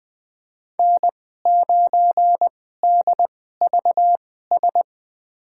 Morse code, heard as a keyed tone, about 20 wpm, spaced as for 15 wpm, 700 Hz. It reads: N9DVS
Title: USA Call Signs - 5 Characters - 2x Character Spacing - 20wpm